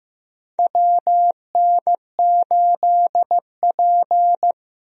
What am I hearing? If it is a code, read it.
Morse code, heard as a keyed tone, about 15 wpm, 700 Hz. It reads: WN8P
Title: USA Call Signs - 4 Characters - Repeated 3x - 15wpm